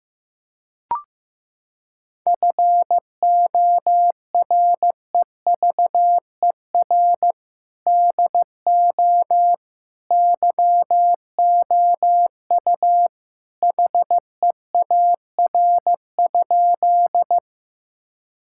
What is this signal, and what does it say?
Morse code, heard as a keyed tone, about 16 wpm, 700 Hz. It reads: FOREVER DO YOU HEAR?
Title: Sentences from Top 1600 Words 15wpm